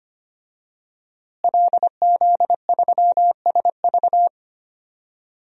Morse code, heard as a keyed tone, about 25 wpm, 700 Hz. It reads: LZ3SV
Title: DXCC-100 Call Signs - 5 Characters - Repeated 2x - 25wpm